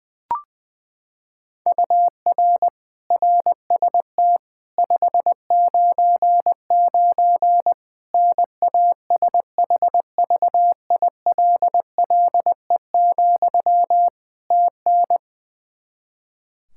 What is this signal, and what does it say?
Morse code, heard as a keyed tone, about 20 wpm, 700 Hz. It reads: UR RST 599 NASHVILLE, TN
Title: Practice QSO - Exchange 1 - Beginning - 20wpm